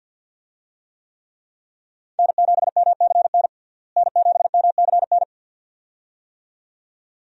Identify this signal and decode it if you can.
Morse code, heard as a keyed tone, about 52 wpm, 700 Hz. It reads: N7MCN N7MCN
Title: USA Call Signs - 5 Characters - Repeated 2x - 50wpm